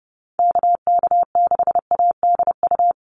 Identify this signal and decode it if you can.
Morse code, heard as a keyed tone, about 30 wpm, 700 Hz. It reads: KK6ADU
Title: USA Call Signs 30wpm